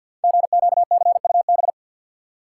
Morse code, heard as a keyed tone, about 50 wpm, 700 Hz. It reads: GQCUB
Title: Code Groups - 5 Letter:Numbers - 50wpm